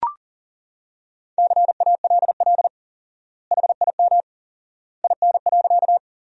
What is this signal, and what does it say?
Morse code, heard as a keyed tone, about 40 wpm, 700 Hz. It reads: CALL HIM IN.
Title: 4x Word-Spacing - Sentences from Top 100 Words - 40wpm